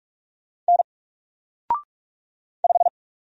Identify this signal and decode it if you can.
Morse code, heard as a keyed tone, about 44 wpm, 700 Hz. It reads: N 5
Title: Single Letter-Number 45wpm